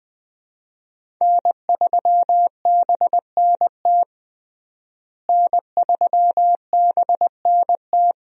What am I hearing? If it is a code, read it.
Morse code, heard as a keyed tone, about 20 wpm, 700 Hz. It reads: N3BNT N3BNT
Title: USA Call Signs - 5 Characters - Repeated 2x - 20wpm